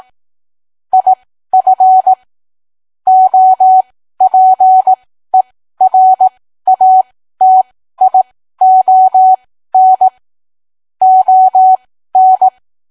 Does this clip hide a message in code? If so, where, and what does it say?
Yes: Morse code from 0.9 to 12.5 s, heard as a keyed tone, about 18 wpm, spaced as for 13 wpm, 750 Hz. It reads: IF OPERATION ON